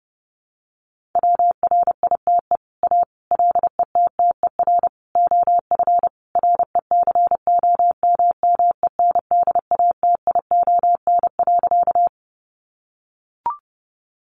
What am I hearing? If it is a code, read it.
Morse code, heard as a keyed tone, about 28 wpm, 700 Hz. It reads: WRITE A LETTER OF RECOMMENDATION.
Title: Sentences from Top 1400 Words 30wpm